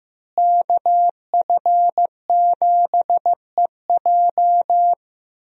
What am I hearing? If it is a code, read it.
Morse code, heard as a keyed tone, about 15 wpm, 700 Hz. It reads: KF7EJ
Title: USA Call Signs - 5 Characters - Repeated 3x - 15wpm